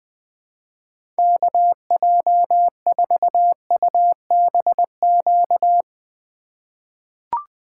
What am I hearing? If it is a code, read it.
Morse code, heard as a keyed tone, about 21 wpm, 700 Hz. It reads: KJ4UBQ
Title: USA Call Signs - 2x Character Spacing - 20wpm